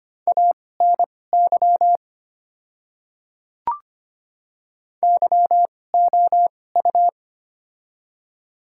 Morse code, heard as a keyed tone, about 25 wpm, spaced as for 18 wpm, 690 Hz. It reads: ANY YOU
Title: 2x Character Spacing - Top 200 Words - 25wpm